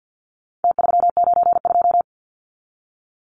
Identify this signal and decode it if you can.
Morse code, heard as a keyed tone, about 50 wpm, 700 Hz. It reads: T392
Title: Numbers - Contest Style - 4-Digit 50wpm